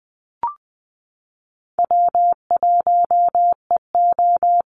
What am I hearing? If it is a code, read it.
Morse code, heard as a keyed tone, about 20 wpm, 700 Hz. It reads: W1EO
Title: USA Call Signs Repeated 3x 20wpm